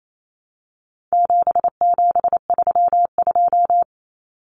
Morse code, heard as a keyed tone, about 28 wpm, 700 Hz. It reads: 7732
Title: Common Cut Numbers - 4-Digit 28wpm